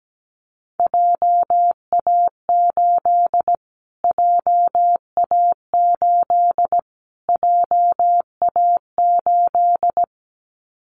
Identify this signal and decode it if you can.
Morse code, heard as a keyed tone, about 17 wpm, 700 Hz. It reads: JA8 JA8 JA8
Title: Callsign Prefixes for Top 25 DXCC - Repeated 3x - 17wpm